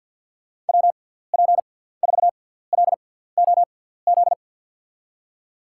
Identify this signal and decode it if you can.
Morse code, heard as a keyed tone, about 53 wpm, spaced as for 17 wpm, 700 Hz. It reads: WP4LOZ
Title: USA Call Signs - 6x Character Spacing - 50wpm